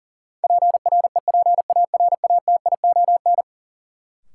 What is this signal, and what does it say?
Morse code, heard as a keyed tone, about 40 wpm, 700 Hz. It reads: PREPARATION
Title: Sentences from Top 1400 Words - New Word Drill - 40wpm